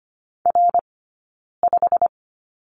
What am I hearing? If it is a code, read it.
Morse code, heard as a keyed tone, about 25 wpm, spaced as for 9 wpm, 690 Hz. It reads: R5
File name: Lesson 15d - Number 5 - Two Characters 25wpm